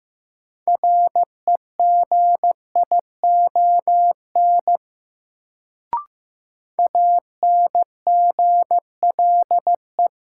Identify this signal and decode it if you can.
Morse code, heard as a keyed tone, about 16 wpm, 700 Hz. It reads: REGION ANGLE
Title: Sentences from Top 1500 Words - New Word Drill Alt - 15wpm